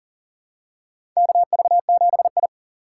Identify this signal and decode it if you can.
Morse code, heard as a keyed tone, about 40 wpm, 700 Hz. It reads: KV7I
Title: USA Call Signs - 4 Characters - 6x Character Spacing - 40wpm